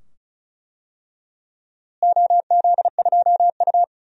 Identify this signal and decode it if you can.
Morse code, heard as a keyed tone, about 36 wpm, 700 Hz. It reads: OZ2U
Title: DXCC-100 Call Signs - 4 Characters - Repeated 2x - 35wpm